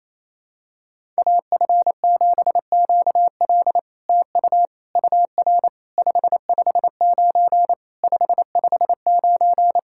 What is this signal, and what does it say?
Morse code, heard as a keyed tone, about 28 wpm, 700 Hz. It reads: AF7QL TU UR 559 559
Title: SOTA QSO Practice 28wpm